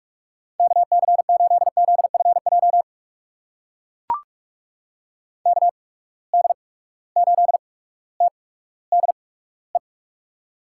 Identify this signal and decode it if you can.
Morse code, heard as a keyed tone, about 44 wpm, spaced as for 19 wpm, 700 Hz. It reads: KC8ZFJ KD8TDE
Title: USA Call Signs - 6 Characters - 8x Character Spacing - 45wpm